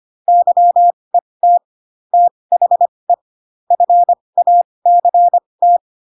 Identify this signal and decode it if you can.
Morse code, heard as a keyed tone, about 25 wpm, spaced as for 20 wpm, 700 Hz. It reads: YET THE FACT
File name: Pinocchio / Pinocchio ch1 15 25 ik1hge